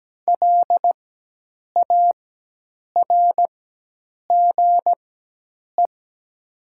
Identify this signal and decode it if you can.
Morse code, heard as a keyed tone, about 17 wpm, spaced as for 8 wpm, 700 Hz. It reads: LARGE
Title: 4x Character Spacing - Top 200 Words - 17wpm